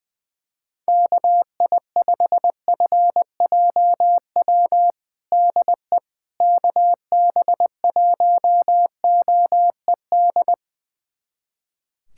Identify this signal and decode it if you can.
Morse code, heard as a keyed tone, about 21 wpm, 700 Hz. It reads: KI5FJW DE KB1OED
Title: USA Callsign DE Callsign - 6 Characters - 20wpm